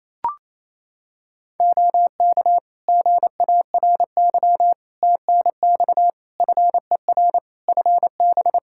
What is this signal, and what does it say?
Morse code, heard as a keyed tone, about 28 wpm, 700 Hz. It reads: OK GARY TNX FER FB
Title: Practice QSO - Exchange 4 - Ending - 28wpm